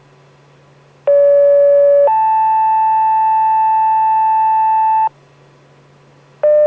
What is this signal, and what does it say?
Two-tone page, heard as a one-second tone, then a three-second tone, about 570 then 880 Hz. It is heard from 1.1 to 5.1 s.